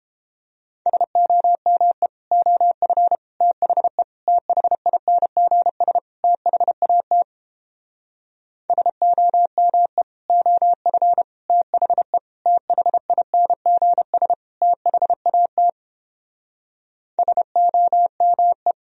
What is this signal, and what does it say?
Morse code, heard as a keyed tone, about 29 wpm, 700 Hz. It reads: SOME OF THE THINGS THAT SOME OF THE THINGS THAT SOME